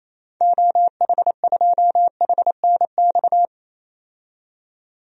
Morse code, heard as a keyed tone, about 28 wpm, 700 Hz. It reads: OH2HNX
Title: DXCC-100 Call Signs - 6 Characters - 28wpm